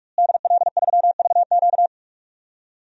Morse code, heard as a keyed tone, about 46 wpm, 700 Hz. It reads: DL3VQ
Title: DXCC-100 Call Signs - 5 Characters - Repeated 2x - 45wpm